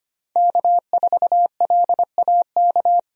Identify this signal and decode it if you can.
Morse code, heard as a keyed tone, about 25 wpm, 700 Hz. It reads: K4LAK
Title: USA Call Signs - 5 Characters - 25wpm